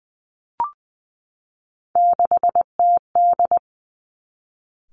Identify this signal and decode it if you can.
Morse code, heard as a keyed tone, about 20 wpm, 700 Hz. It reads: 6TD
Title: Three Challenging Letter-Number Combinations 20wpm